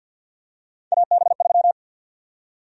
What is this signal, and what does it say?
Morse code, heard as a keyed tone, about 53 wpm, 700 Hz. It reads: AB3
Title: USA Call Sign Prefixes - 50wpm